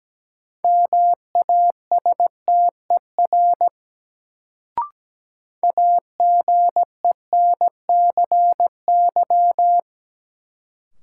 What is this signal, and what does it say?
Morse code, heard as a keyed tone, about 17 wpm, 700 Hz. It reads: MASTER AGENCY
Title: Sentences from Top 1300 Words - New Word Drill - 17wpm